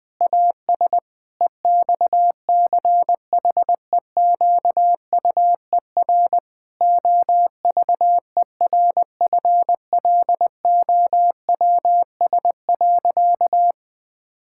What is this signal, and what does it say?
Morse code, heard as a keyed tone, about 19 wpm, 700 Hz. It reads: AS EXCHEQUER OVERFLOWS.
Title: Pangram Challenge v2 - 20wpm